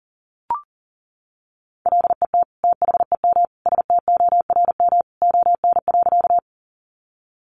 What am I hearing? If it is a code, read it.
Morse code, heard as a keyed tone, about 40 wpm, 700 Hz. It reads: LET THEM STORM ON.
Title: Sentences from Top 1500 Words 40wpm